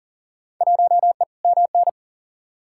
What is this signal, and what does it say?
Morse code, heard as a keyed tone, about 40 wpm, 700 Hz. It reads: 1E MN
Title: Field Day Exchanges - 40wpm